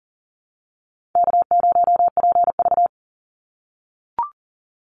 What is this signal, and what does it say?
Morse code, heard as a keyed tone, about 40 wpm, 700 Hz. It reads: K0PV
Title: Call Signs - 40wpm